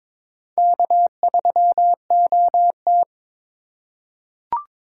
Morse code, heard as a keyed tone, about 22 wpm, 700 Hz. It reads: K3OT
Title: USA Call Signs - 4 Characters - 4x Character Spacing - 22wpm